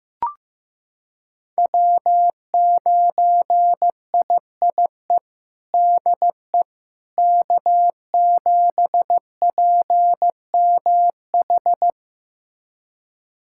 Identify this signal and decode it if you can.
Morse code, heard as a keyed tone, about 15 wpm, 700 Hz. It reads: W9IIE DE K7PMH
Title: USA Callsign DE Callsign - 5 Characters - 15wpm